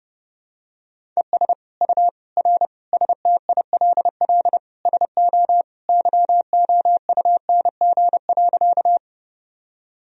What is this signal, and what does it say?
Morse code, heard as a keyed tone, about 30 wpm, 700 Hz. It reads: ES U R STILL SO YOUNG.
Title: Sentences from Top 400 Words and Abbreviations 30wpm